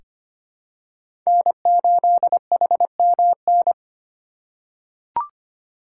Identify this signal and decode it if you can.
Morse code, heard as a keyed tone, about 25 wpm, 700 Hz. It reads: N8HMN
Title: USA Call Signs - 5 Characters - 2x Character Spacing - 25wpm